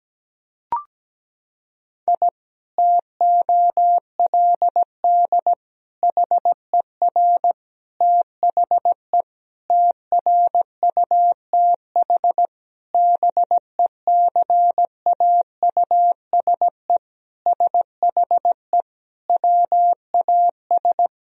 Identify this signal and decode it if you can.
Morse code, heard as a keyed tone, about 17 wpm, 700 Hz. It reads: I TOLD HER THE TRUTH BECAUSE SHE WAS
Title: Sentences from Top 1000 Words - Review Part 3 - 17wpm